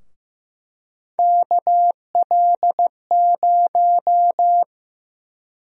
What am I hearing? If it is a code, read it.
Morse code, heard as a keyed tone, about 15 wpm, 700 Hz. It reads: KL0
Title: USA Call Sign Prefixes - 15wpm